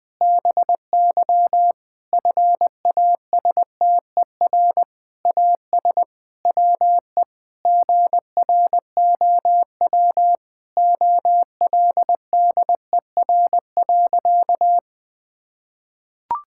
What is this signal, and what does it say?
Morse code, heard as a keyed tone, about 20 wpm, 700 Hz. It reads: BY FASTER AS WE GROW OLDER.
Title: Difficult Sentences with Hints - 20wpm